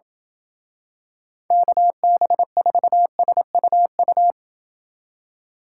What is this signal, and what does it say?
Morse code, heard as a keyed tone, about 28 wpm, 700 Hz. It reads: KB4SUU